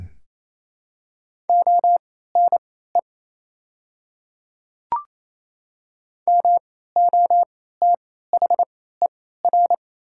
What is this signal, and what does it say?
Morse code, heard as a keyed tone, about 28 wpm, spaced as for 16 wpm, 700 Hz. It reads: ONE MOTHER
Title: 3x Character Spacing - Top 200 Words - 28wpm